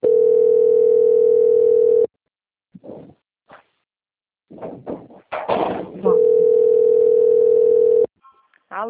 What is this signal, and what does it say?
Ringback tone (North America), 440 + 480 Hz, with speech.
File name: out- - - - - .12355